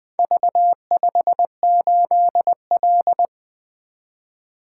Morse code, heard as a keyed tone, about 23 wpm, 700 Hz. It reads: V58L
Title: Speed-Racing - Four Letter-Number Combinations - 20wpm